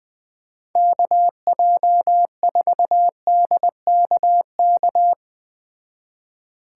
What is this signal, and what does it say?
Morse code, heard as a keyed tone, about 21 wpm, 700 Hz. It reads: KJ4DKK